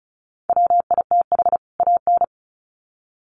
Morse code, heard as a keyed tone, about 34 wpm, 700 Hz. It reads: WITH AN
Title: Sets of 2 Words 35wpm